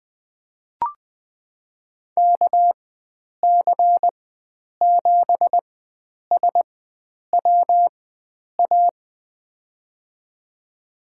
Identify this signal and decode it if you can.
Morse code, heard as a keyed tone, about 20 wpm, spaced as for 9 wpm, 700 Hz. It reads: KC7SWA
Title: USA Call Signs - 6 Characters - 4x Character Spacing - 20wpm